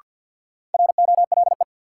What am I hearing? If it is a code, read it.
Morse code, heard as a keyed tone, about 51 wpm, 700 Hz. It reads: ROLE
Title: Sentences from Top 1500 Words - New Word Drill Alt - 50wpm